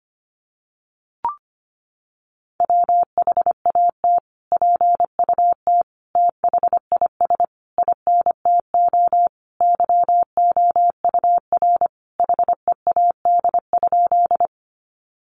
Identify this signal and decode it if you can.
Morse code, heard as a keyed tone, about 25 wpm, 700 Hz. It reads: WHAT PUT THIS INTO YOUR HEAD?
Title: Sentences from Top 600 Words 25wpm